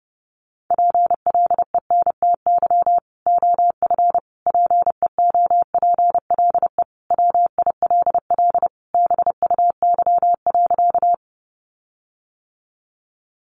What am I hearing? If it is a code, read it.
Morse code, heard as a keyed tone, about 28 wpm, 700 Hz. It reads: PLENTY OF PEOPLE WILL BUY.
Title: Sentences from Top 1300 Words 30wpm